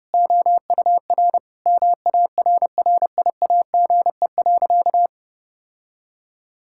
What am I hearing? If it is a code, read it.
Morse code, heard as a keyed tone, about 30 wpm, 700 Hz. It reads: OUR MARRIAGE.
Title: Sentences from Top 1000 Words - Review Part 5 - 30wpm